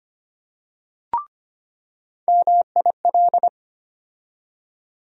Morse code, heard as a keyed tone, about 25 wpm, 700 Hz. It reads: MIL
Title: USA Call Sign Postfixes - 3 Letters 25wpm